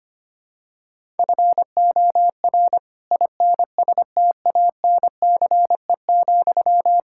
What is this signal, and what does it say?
Morse code, heard as a keyed tone, about 25 wpm, 700 Hz. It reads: FOR INSTANCE,
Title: Sentences from Top 1000 Words - Review Part 5 - 25wpm